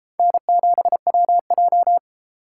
Morse code, heard as a keyed tone, about 33 wpm, 700 Hz. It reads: N7WJ